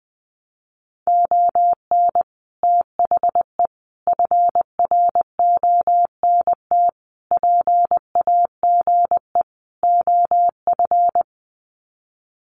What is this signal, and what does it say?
Morse code, heard as a keyed tone, about 20 wpm, 700 Hz. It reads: ON THE FRONT PAGE OF
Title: Sets of 5 Words Encore 20wpm